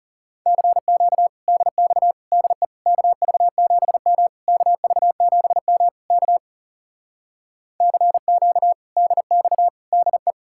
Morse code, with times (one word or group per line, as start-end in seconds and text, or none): 0.5-1.3 CQ
1.5-2.1 DX
2.3-2.6 DE
2.9-4.3 KV7M
4.5-5.9 KV7M
6.1-6.4 K
7.8-8.7 CQ
9.0-9.7 DX
9.9-10.3 DE